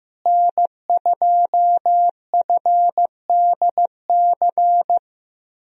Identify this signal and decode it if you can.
Morse code, heard as a keyed tone, about 16 wpm, 700 Hz. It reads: N2FDC
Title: USA Call Signs - 5 Characters - Repeated 2x - 15wpm